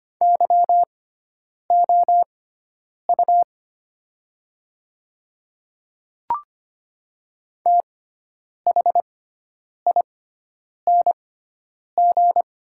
Morse code, heard as a keyed tone, about 25 wpm, spaced as for 9 wpm, 700 Hz. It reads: YOU THING
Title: 6x Character Spacing - Top 100 Words - 25wpm